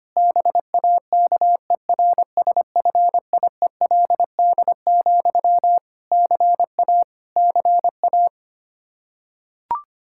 Morse code, heard as a keyed tone, about 25 wpm, 700 Hz. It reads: BAKERSFIELD, CA CA
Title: K1USN SST QSO Practice 25wpm